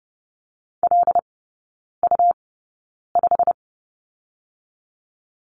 Morse code, heard as a keyed tone, about 30 wpm, spaced as for 9 wpm, 690 Hz. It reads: LU5
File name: Lesson 17e - Letter u - Three Characters 30wpm